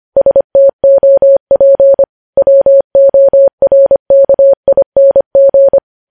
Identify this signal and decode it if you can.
Morse code, heard as a keyed tone, about 25 wpm, 550 Hz. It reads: STOP WORKING
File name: podcast / News headlines-1x-25x25